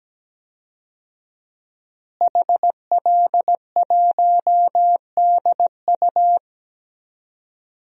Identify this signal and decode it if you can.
Morse code, heard as a keyed tone, about 17 wpm, 700 Hz. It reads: HL1DU
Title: DXCC-100 Call Signs - 5 Characters - Repeated 2x - 17wpm